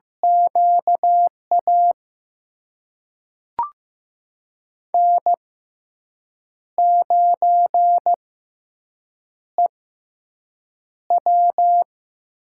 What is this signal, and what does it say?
Morse code, heard as a keyed tone, about 15 wpm, spaced as for 6 wpm, 700 Hz. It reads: QA N9EW
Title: USA Call Signs - 4 Characters - 6x Character Spacing - 15wpm